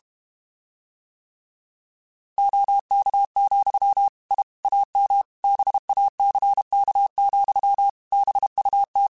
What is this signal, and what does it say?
Morse code, heard as a keyed tone, about 32 wpm, 800 Hz. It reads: OK, I AM BACK, BUT
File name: QSOs / QSOs-mp3s / kc7sac@30